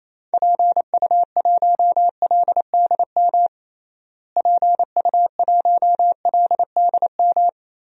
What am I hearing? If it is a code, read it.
Morse code, heard as a keyed tone, about 27 wpm, 700 Hz. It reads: PU1LDM PU1LDM
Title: DXCC-100 Call Signs - 6 Characters - Repeated 3x - 28wpm